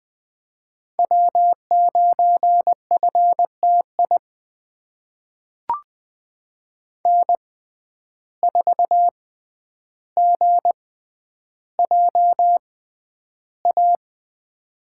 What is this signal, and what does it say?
Morse code, heard as a keyed tone, about 20 wpm, spaced as for 10 wpm, 700 Hz. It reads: W9FTI N4GJA
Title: USA Call Signs - 5 Characters - 6x Character Spacing - 20wpm